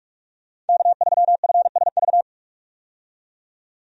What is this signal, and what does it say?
Morse code, heard as a keyed tone, about 46 wpm, 700 Hz. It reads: K3FSV
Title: USA Call Signs - 5 Characters - 45wpm